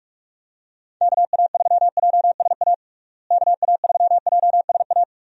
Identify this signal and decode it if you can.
Morse code, heard as a keyed tone, about 41 wpm, 700 Hz. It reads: KA3JSA KA3JSA
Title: USA Call Signs - 6 Characters - Repeated 3x - 45wpm